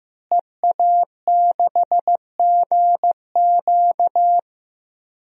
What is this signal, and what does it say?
Morse code, heard as a keyed tone, about 15 wpm, 700 Hz. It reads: EA6GQ
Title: DXCC-100 Call Signs - 5 Characters - Repeated 2x - 15wpm